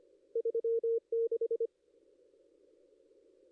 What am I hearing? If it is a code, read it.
Morse code, heard as a keyed tone, about 24 wpm, 440 Hz. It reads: 36
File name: contests / ss-cw-k5tr-op-n5rz -2300